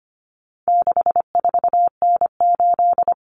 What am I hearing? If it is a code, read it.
Morse code, heard as a keyed tone, about 25 wpm, 700 Hz. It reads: 64N8
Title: Common Cut Numbers - 4-Digit 25wpm